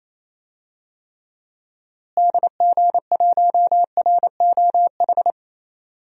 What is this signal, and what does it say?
Morse code, heard as a keyed tone, about 27 wpm, 700 Hz. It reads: DG1ROH
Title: DXCC-100 Call Signs - 6 Characters - Repeated 2x - 28wpm